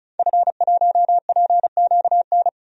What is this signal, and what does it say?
Morse code, heard as a keyed tone, about 34 wpm, 700 Hz. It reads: F1PQN
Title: DXCC-100 Call Signs - 5 Characters - Repeated 3x - 35wpm